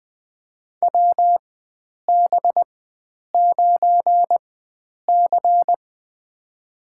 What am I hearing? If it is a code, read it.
Morse code, heard as a keyed tone, about 20 wpm, spaced as for 9 wpm, 700 Hz. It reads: WB9C
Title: USA Call Signs - 4 Characters - 4x Character Spacing - 20wpm